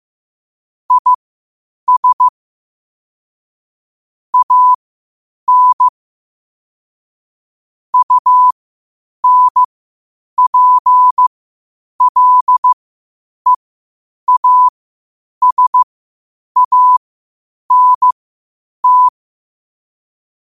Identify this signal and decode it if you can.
Morse code, heard as a keyed tone, about 15 wpm, spaced as for 8 wpm, 1000 Hz. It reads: IS AN UNPLEASANT
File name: qotdfiles / 8wpm